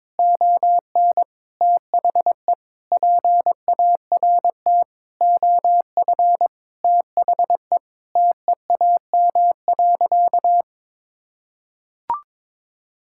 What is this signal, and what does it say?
Morse code, heard as a keyed tone, about 21 wpm, 700 Hz. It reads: ON THE PART OF THE TEAM.
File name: practice / 22wpm / Sentences with 4-Word Phrases - 22wpm